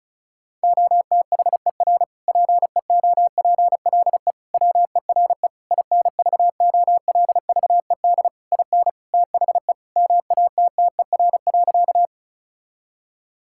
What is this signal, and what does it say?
Morse code, heard as a keyed tone, about 36 wpm, 700 Hz. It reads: OTHER PEOPLE WERE INVOLVED IN THE MATTER.
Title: Sentences from Top 1000 Words - Review Part 5 - 35wpm